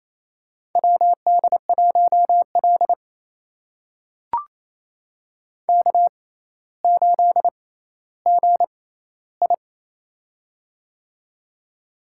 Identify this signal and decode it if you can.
Morse code, heard as a keyed tone, about 28 wpm, spaced as for 14 wpm, 700 Hz. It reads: WD1L K8GI
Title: USA Call Signs - 4 Characters - 6x Character Spacing - 28wpm